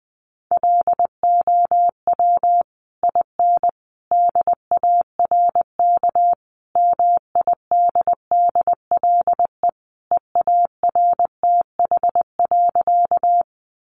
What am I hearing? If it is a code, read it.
Morse code, heard as a keyed tone, about 20 wpm, 700 Hz. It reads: LOW IN DARK MIDDLE EARTH.
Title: Sentences from Top 900 Words 20wpm